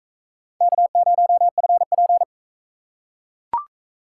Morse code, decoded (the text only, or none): K0FP